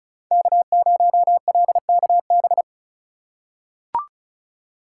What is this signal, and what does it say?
Morse code, heard as a keyed tone, about 36 wpm, 700 Hz. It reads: K0LKB
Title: USA Call Signs - 5 Characters - 3x Character Spacing - 35wpm